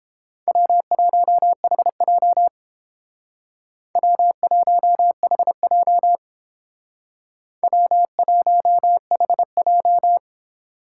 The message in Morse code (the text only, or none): W1HJ W1HJ W1HJ